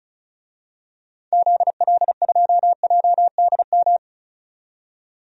Morse code, heard as a keyed tone, about 36 wpm, 690 Hz. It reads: ZL2JDM
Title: DXCC-100 Call Signs - 6 Characters - 35wpm